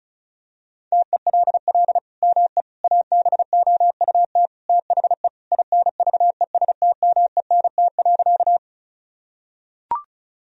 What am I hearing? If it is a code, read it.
Morse code, heard as a keyed tone, about 34 wpm, 700 Hz. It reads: TELL ME ABOUT THE INVESTMENT.
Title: Sentences from Top 1000 Words - Review Part 4 - 35wpm